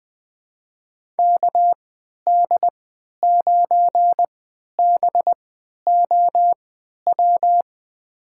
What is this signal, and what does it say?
Morse code, heard as a keyed tone, about 20 wpm, spaced as for 11 wpm, 700 Hz. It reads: KD9BOW
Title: USA Call Signs - 6 Characters - 3x Character Spacing - 20wpm